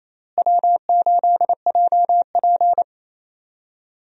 Morse code, heard as a keyed tone, about 28 wpm, 700 Hz. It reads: W8JP
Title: USA Call Signs - 4 Characters - Repeated 3x - 28wpm